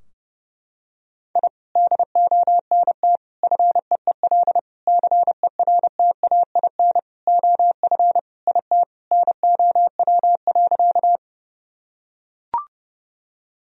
Morse code, heard as a keyed tone, about 30 wpm, 700 Hz. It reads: I DONT FEEL CERTAIN OF IT NOW.
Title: Sentences from Top 1000 Words - Review Part 2 - 30wpm